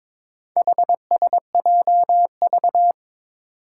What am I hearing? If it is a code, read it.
Morse code, heard as a keyed tone, about 22 wpm, 700 Hz. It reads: HSJV
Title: Four Challenging Letter-Number Combinations 22wpm